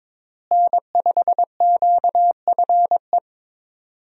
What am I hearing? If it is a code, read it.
Morse code, heard as a keyed tone, about 22 wpm, 700 Hz. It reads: N5QFE
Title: USA Call Signs - 5 Characters - Repeated 3x - 22wpm